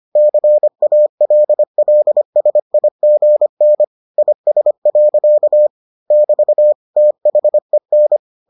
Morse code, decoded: CALLSIGN IS. <BT> THEN

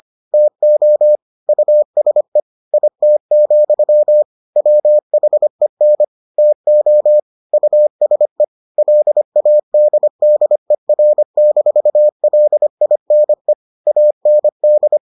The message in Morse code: TO USE IT, WHEN TO USE LADDER-LINE AND

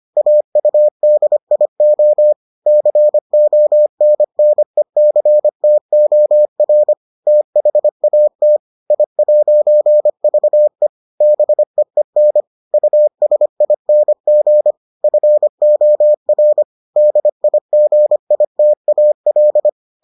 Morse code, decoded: AUDIO CONNECTOR THAT I'VE BEEN USING FOR DIGITAL

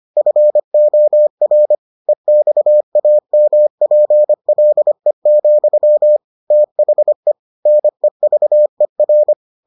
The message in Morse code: FOR EXAMPLE, THE NEVER

